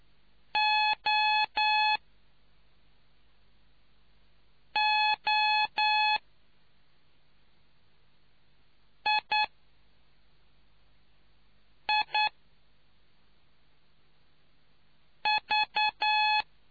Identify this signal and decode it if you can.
Morse code, heard as a keyed tone, about 9 wpm, spaced as for 3 wpm, 850 Hz. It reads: OOIIV